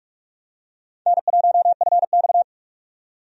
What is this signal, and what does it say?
Morse code, heard as a keyed tone, about 46 wpm, 700 Hz. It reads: N1FX
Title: USA Call Signs - 4 Characters - 45wpm